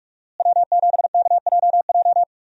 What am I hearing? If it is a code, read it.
Morse code, heard as a keyed tone, about 45 wpm, 700 Hz. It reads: W7KJJ
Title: USA Call Signs - 5 Characters - Repeated 2x - 45wpm